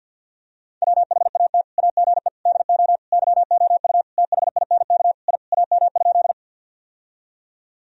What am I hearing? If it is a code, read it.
Morse code, heard as a keyed tone, about 51 wpm, 700 Hz. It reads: WHAT AGE DO YOU THINK I AM?
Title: Sentences from Top 1000 Words - Review Part 3 - 50wpm